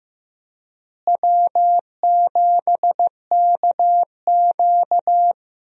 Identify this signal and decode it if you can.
Morse code, heard as a keyed tone, about 15 wpm, 700 Hz. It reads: W7KQ
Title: USA Call Signs - 4 Characters - Repeated 3x - 15wpm